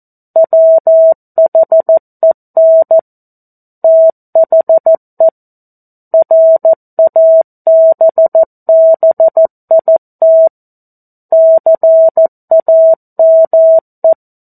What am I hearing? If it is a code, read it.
Morse code, heard as a keyed tone, about 14 wpm, 650 Hz. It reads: WHEN THE RABBIT CAME
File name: 02-Chapter-02 14 WPM